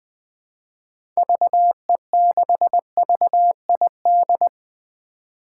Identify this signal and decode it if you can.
Morse code, heard as a keyed tone, about 20 wpm, 700 Hz. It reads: VE6VID
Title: 2023 SOTA Chaser Call Signs - 20wpm